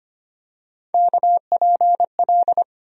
Morse code, heard as a keyed tone, about 25 wpm, 700 Hz. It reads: KPL
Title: USA Call Sign Postfixes - 3 Letters 25wpm